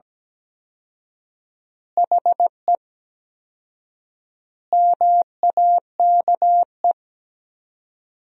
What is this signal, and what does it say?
Morse code, heard as a keyed tone, about 17 wpm, 700 Hz. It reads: HE MAKE
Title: 4x Word-Spacing - Sentences from Top 400 Words - 17wpm